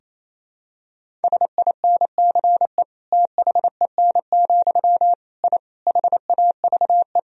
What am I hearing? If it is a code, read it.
Morse code, heard as a keyed tone, about 28 wpm, 700 Hz. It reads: SINCE THEN, I HAVE
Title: Sentences from Top 1000 Words - Review Part 1 - 28wpm